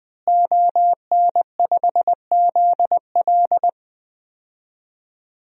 Morse code, heard as a keyed tone, about 20 wpm, 700 Hz. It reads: ON5ZL
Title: DXCC-100 Call Signs - 5 Characters - Repeated 2x - 20wpm